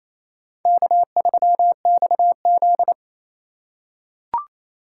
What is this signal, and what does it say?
Morse code, heard as a keyed tone, about 28 wpm, 700 Hz. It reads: K3XZ
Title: USA Call Signs - 4 Characters - 2x Character Spacing - 28wpm